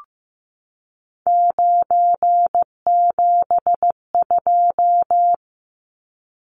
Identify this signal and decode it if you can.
Morse code, heard as a keyed tone, about 15 wpm, 700 Hz. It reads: 972
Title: Numbers - 3-Digit 15wpm